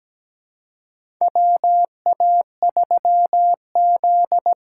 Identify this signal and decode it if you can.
Morse code, heard as a keyed tone, about 17 wpm, 700 Hz. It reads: WA3Z
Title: USA Call Signs - 6 Characters - 6x Character Spacing - 17wpm